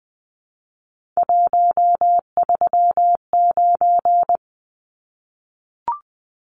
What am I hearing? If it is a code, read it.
Morse code, heard as a keyed tone, about 20 wpm, 700 Hz. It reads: 139
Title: Numbers - 3-Digit 20wpm